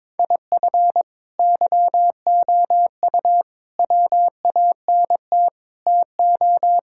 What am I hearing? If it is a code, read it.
Morse code, heard as a keyed tone, about 22 wpm, 700 Hz. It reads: IF YOU WANT TO
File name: Sentences with 7-Word Phrases - 22wpm